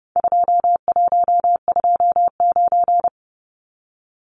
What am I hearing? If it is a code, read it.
Morse code, heard as a keyed tone, about 29 wpm, 700 Hz. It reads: 2129
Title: Numbers - 4-Digit 30wpm